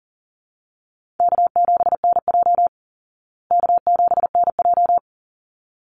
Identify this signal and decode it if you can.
Morse code, heard as a keyed tone, about 37 wpm, 700 Hz. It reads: K7NJ K7NJ
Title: Easy Call Signs Repeated 2x - 40wpm